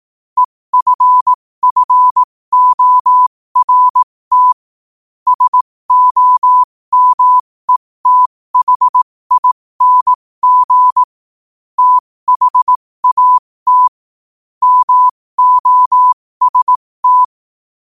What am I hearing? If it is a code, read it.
Morse code, heard as a keyed tone, about 18 wpm, spaced as for 15 wpm, 1000 Hz. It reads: EFFORT SOMETHING THAT MOST
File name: qotdfiles / 15wpm